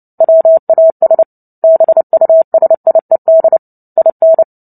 Morse code, heard as a keyed tone, about 30 wpm, 650 Hz. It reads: WAS BUSIED IN
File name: Book Two / 03-Chapter 3 - 29 WPM